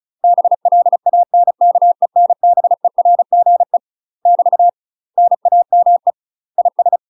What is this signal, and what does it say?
Morse code, heard as a keyed tone, about 34 wpm, 700 Hz. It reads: BLANKENBERGE <BT> NAME IS
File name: UK MASTER QSOs TRAINING / 35WPM / UK MASTER QSOs 0043